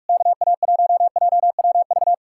Morse code, heard as a keyed tone, about 64 wpm, spaced as for 54 wpm, 700 Hz. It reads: KA1JWV
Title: Speed-Racing Common QSO Phrases - 45wpm